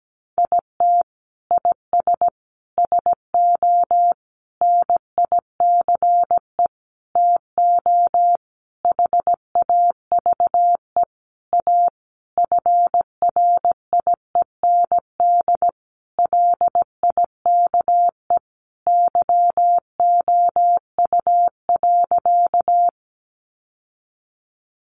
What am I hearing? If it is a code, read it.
Morse code, heard as a keyed tone, about 17 wpm, 700 Hz. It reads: IT IS SO NICE TO HAVE A FRIEND LIKE YOU.
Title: Sentences from Top 1100 Words 17wpm